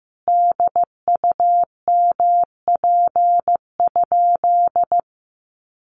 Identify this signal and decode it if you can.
Morse code, heard as a keyed tone, about 15 wpm, 700 Hz. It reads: DUMP?
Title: Sentences from Top 1300 Words 15wpm